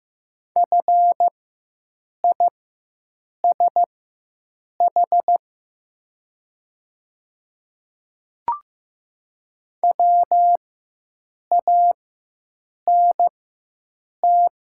Morse code, heard as a keyed tone, about 15 wpm, spaced as for 7 wpm, 700 Hz. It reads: FISH WANT
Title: 4x Character Spacing - Top 300 Words - 15wpm